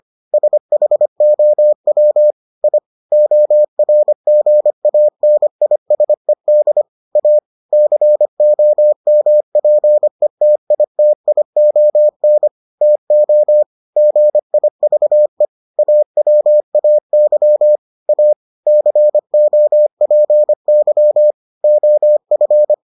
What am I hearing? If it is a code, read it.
Morse code, heard as a keyed tone, about 25 wpm, 600 Hz. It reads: SHOW I ORGANISED A COMPETITION TO GIVE AWAY A COPY OF